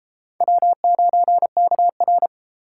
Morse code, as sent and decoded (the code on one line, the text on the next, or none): .-- ----. -.- .-.
W9KR